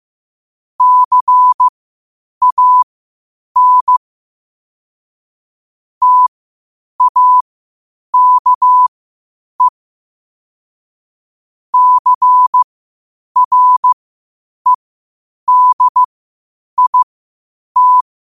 Morse code, heard as a keyed tone, about 15 wpm, spaced as for 8 wpm, 1000 Hz. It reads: CAN TAKE CREDIT
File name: qotdfiles / 8wpm